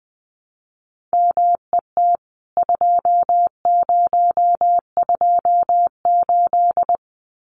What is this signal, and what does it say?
Morse code, heard as a keyed tone, about 20 wpm, 700 Hz. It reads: MET 2028
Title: CWT Exchange 20wpm